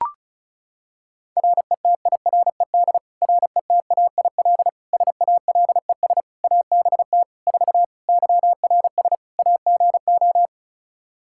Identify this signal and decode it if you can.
Morse code, heard as a keyed tone, about 34 wpm, 700 Hz. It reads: RETIRED RETAIL SALES ABT 4 YRS AGO
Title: Practice QSO - Exchange 3 - More Info - 35wpm